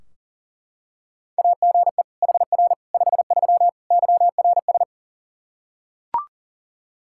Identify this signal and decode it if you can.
Morse code, heard as a keyed tone, about 40 wpm, 700 Hz. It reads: AGE HR 53 YRS
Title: Practice QSO - Exchange 3 - More Info - 40wpm